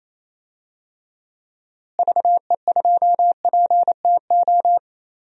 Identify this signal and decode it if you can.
Morse code, heard as a keyed tone, about 28 wpm, 700 Hz. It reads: VE2PTO
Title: DXCC-100 Call Signs - 6 Characters - Repeated 3x - 28wpm